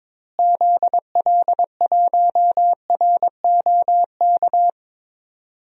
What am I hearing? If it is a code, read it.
Morse code, heard as a keyed tone, about 22 wpm, 700 Hz. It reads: ZL1ROK
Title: DXCC-100 Call Signs - 6 Characters - Repeated 3x - 22wpm